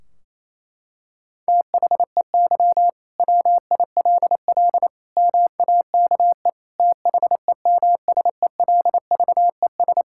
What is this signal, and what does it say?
Morse code, heard as a keyed tone, about 28 wpm, 700 Hz. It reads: THEY WILL MAKE THEMSELVES
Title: Sentences from Top 1600 Words 28wpm